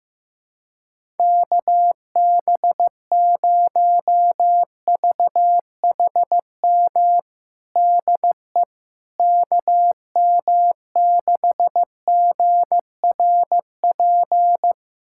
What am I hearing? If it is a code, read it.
Morse code, heard as a keyed tone, about 15 wpm, 700 Hz. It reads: KB0VHM DE KM6GRP
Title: USA Callsign DE Callsign - 6 Characters - 15wpm